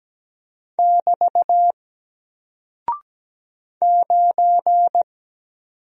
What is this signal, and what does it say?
Morse code, heard as a keyed tone, about 17 wpm, 700 Hz. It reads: <BT> 9
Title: Single Character (Letter:Number:Punctuation:Prosign) Alt 17wpm